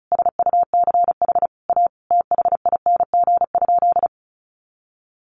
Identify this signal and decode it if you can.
Morse code, heard as a keyed tone, about 34 wpm, 700 Hz. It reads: SUCH A THING?
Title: Sentences from Top 200 Words 35wpm